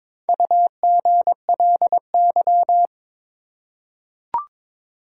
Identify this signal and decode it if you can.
Morse code, heard as a keyed tone, about 22 wpm, 700 Hz. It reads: UGLY
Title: Four Letter Words - 22wpm